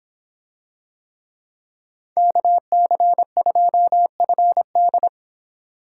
Morse code, heard as a keyed tone, about 26 wpm, 700 Hz. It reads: KC2FD